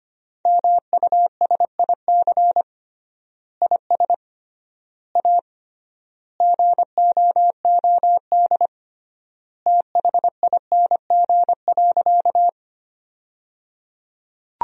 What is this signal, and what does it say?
Morse code, heard as a keyed tone, about 25 wpm, 700 Hz. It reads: MUSIC IS A GOOD THING.
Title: 3x Word-Spacing - Sentences from Top 300 Words - 25wpm